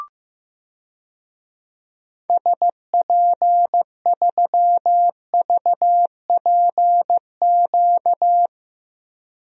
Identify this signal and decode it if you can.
Morse code, heard as a keyed tone, about 15 wpm, 700 Hz. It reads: SP3VPQ
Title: DXCC-100 Call Signs - 6 Characters - 15wpm